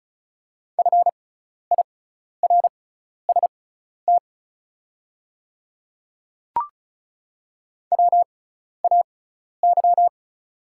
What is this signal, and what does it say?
Morse code, heard as a keyed tone, about 34 wpm, spaced as for 12 wpm, 690 Hz. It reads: FIRST WAY
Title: 6x Character Spacing - Top 200 Words - 35wpm